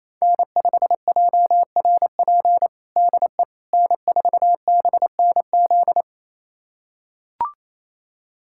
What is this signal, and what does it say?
Morse code, heard as a keyed tone, about 27 wpm, 690 Hz. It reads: N5JRP DE N4BNZ
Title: USA Callsign DE Callsign - 5 Characters - 28wpm